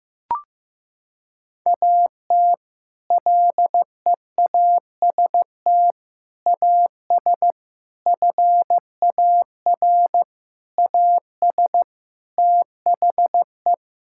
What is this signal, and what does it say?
Morse code, heard as a keyed tone, about 15 wpm, 700 Hz. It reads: AT LEAST AS FAR AS THE
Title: Sentences from Top 1000 Words - Review Part 6 - 15wpm